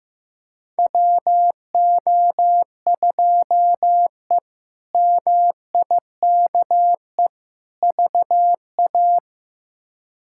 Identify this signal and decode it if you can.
Morse code, heard as a keyed tone, about 16 wpm, 700 Hz. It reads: WO2E MIKE VA
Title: K1USN SST QSO Practice - 15wpm